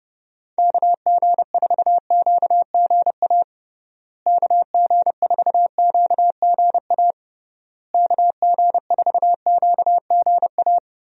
Morse code, heard as a keyed tone, about 32 wpm, 700 Hz. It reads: KG4QGA KG4QGA KG4QGA
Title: USA Call Signs - 6 Characters - Repeated 3x - 30wpm